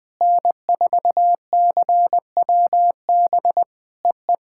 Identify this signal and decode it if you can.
Morse code, heard as a keyed tone, about 20 wpm, 700 Hz. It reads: N4CWB EE
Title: POTA QSO Practice - 20wpm